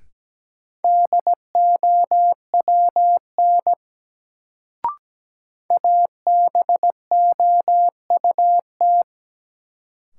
Morse code, heard as a keyed tone, about 17 wpm, 700 Hz. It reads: DOWN ABOUT
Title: Sets of 2 Words - New Word Drill - 17wpm